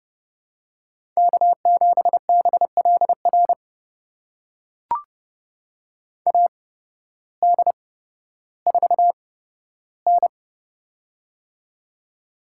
Morse code, heard as a keyed tone, about 31 wpm, spaced as for 14 wpm, 700 Hz. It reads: K7BLR AD4N
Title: USA Call Signs - 8x Character Spacing - 30wpm